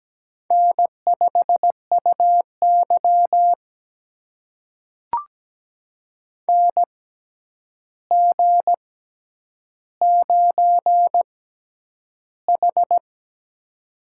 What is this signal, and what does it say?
Morse code, heard as a keyed tone, about 17 wpm, spaced as for 9 wpm, 700 Hz. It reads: N5UY NG9H
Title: USA Call Signs - 4 Characters - 6x Character Spacing - 17wpm